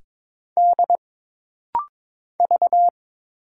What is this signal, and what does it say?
Morse code, heard as a keyed tone, about 22 wpm, 700 Hz. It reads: D V
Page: This rapid-fire timing is geared to push your ICR (Instant Character Recognition) skills.